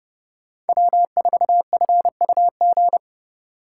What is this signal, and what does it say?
Morse code, heard as a keyed tone, about 30 wpm, 700 Hz. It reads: W4FUG
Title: USA Call Signs - 5 Characters - 3x Character Spacing - 30wpm